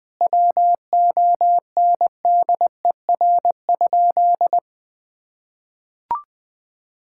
Morse code, heard as a keyed tone, about 20 wpm, 700 Hz. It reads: WONDER?
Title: Sentences from Top 1000 Words - Review Part 5 - 20wpm